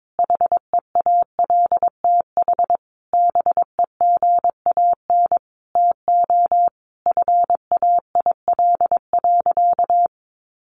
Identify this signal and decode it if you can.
Morse code, heard as a keyed tone, about 22 wpm, 700 Hz. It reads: HEALTH BEGAN TO FAIL.
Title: Sentences from Top 1100 Words 22wpm